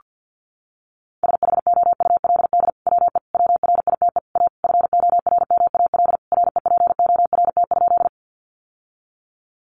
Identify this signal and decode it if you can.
Morse code, heard as a keyed tone, about 50 wpm, 700 Hz. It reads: SHOULD WE WRITE A FORMAL REPORT?
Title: Sentences from Top 1300 Words 50wpm